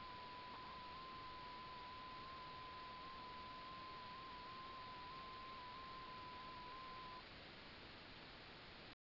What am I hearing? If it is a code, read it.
FT8.